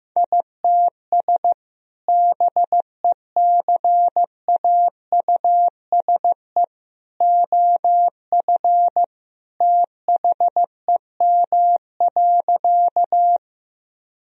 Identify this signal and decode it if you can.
Morse code, heard as a keyed tone, about 15 wpm, 700 Hz. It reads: ITS BECAUSE OF THEM.
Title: Sentences from Top 1000 Words - Review Part 3 - 15wpm